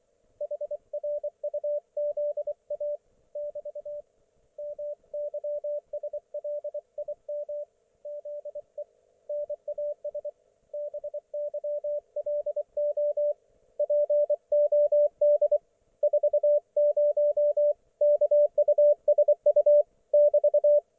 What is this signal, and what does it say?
Morse code, heard as a keyed tone, about 24 wpm, 590 Hz. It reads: HRUZA <BT> MYSLIM ZE NAS BYLO POD 40 KUSU <BT>
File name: Archiv / CUC212 / ok1fhi cul212